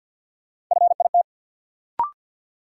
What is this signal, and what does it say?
Morse code, heard as a keyed tone, about 46 wpm, 700 Hz. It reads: FIT
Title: Sentences from Top 800 Words - New Word Drill - 50wpm